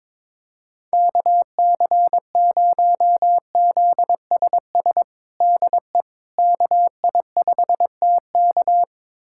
Morse code, heard as a keyed tone, about 22 wpm, 700 Hz. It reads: KC0ZSS DE KI5TK
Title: USA Callsign DE Callsign 22wpm